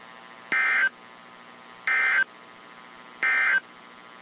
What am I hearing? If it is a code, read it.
Emergency Alert System (SAME) tones, heard as harsh buzzing bursts. It is heard from 0.4 to 3.7 s.
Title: WZ2523 - NOAA Weather Radio